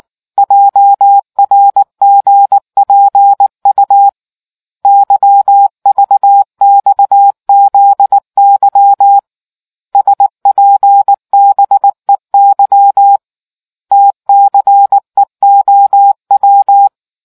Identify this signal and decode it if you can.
Morse code, heard as a keyed tone, about 19 wpm, 800 Hz. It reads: JRGPU YVXZY SPBEY TCEOW